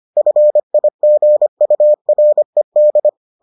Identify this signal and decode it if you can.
Morse code, heard as a keyed tone, about 25 wpm, 600 Hz. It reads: FIGURED